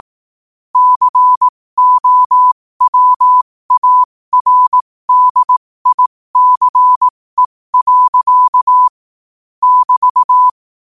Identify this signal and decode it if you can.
Morse code, heard as a keyed tone, about 18 wpm, spaced as for 15 wpm, 1000 Hz. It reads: COWARDICE. <BT>
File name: qotdfiles / 15wpm